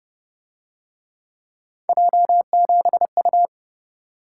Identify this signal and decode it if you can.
Morse code, heard as a keyed tone, about 27 wpm, 700 Hz. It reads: J7U